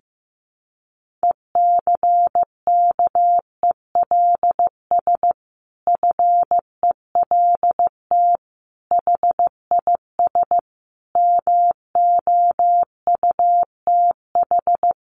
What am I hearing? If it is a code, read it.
Morse code, heard as a keyed tone, about 15 wpm, 700 Hz. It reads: ECKELS FELT HIS MOUTH
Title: A Sound Of Thunder 15wpm